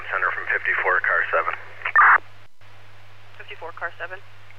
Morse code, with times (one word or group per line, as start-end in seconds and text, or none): none